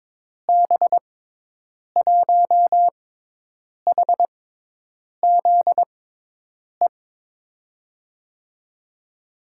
Morse code, heard as a keyed tone, about 22 wpm, spaced as for 8 wpm, 700 Hz. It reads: B1HZE
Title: USA Call Signs - 6 Characters - 6x Character Spacing - 22wpm